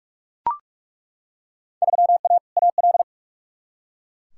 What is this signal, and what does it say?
Morse code, heard as a keyed tone, about 42 wpm, 700 Hz. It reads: 3A AL
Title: Field Day Exchanges - 45wpm